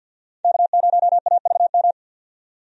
Morse code, heard as a keyed tone, about 48 wpm, 700 Hz. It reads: K0AVM
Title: USA Call Signs - 5 Characters - 50wpm